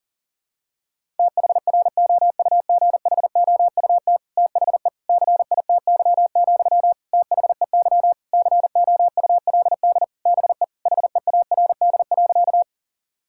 Morse code, heard as a keyed tone, about 37 wpm, 700 Hz. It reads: THROUGHOUT THE CITY, THEY COULD BE HEARD.
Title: Sentences from Top 1000 Words - Review Part 5 - 40wpm